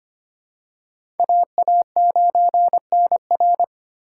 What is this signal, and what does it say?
Morse code, heard as a keyed tone, about 25 wpm, 700 Hz. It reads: AA9NR
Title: USA Call Signs - 5 Characters - 2x Character Spacing - 25wpm